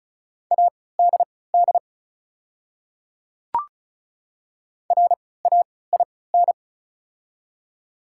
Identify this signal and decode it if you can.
Morse code, heard as a keyed tone, about 36 wpm, spaced as for 20 wpm, 700 Hz. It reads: ADD RAIN
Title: 3x Character Spacing - Top 500 Words - 35wpm